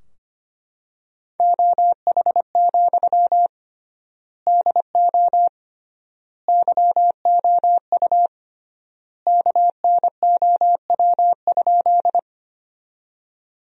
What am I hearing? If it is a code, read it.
Morse code, heard as a keyed tone, about 25 wpm, 700 Hz. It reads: OH, DO YOU KNOW?
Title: 3x Word-Spacing - Sentences from Top 500 Words - 25wpm